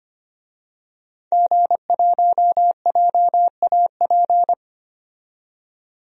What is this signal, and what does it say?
Morse code, heard as a keyed tone, about 25 wpm, 700 Hz. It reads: G1JAP
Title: DXCC-100 Call Signs - 5 Characters - 25wpm